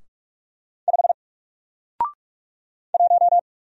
Morse code, heard as a keyed tone, about 46 wpm, 700 Hz. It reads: 5 1
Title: Single Letter-Number 45wpm